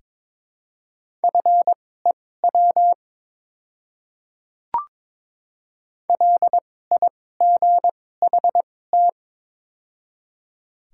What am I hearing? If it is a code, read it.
Morse code, heard as a keyed tone, about 22 wpm, spaced as for 16 wpm, 700 Hz. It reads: FEW LIGHT
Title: 2x Character Spacing - Top 300 Words - 22wpm